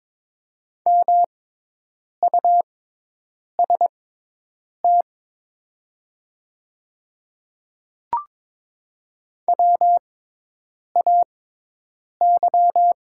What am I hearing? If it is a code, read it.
Morse code, heard as a keyed tone, about 22 wpm, spaced as for 8 wpm, 700 Hz. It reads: MUST WAY
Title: 6x Character Spacing - Top 300 Words - 22wpm